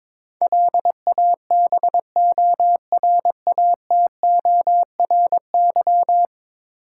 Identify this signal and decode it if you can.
Morse code, heard as a keyed tone, about 22 wpm, 700 Hz. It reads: LABORATORY
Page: morse-code-ninja-practice-22wpm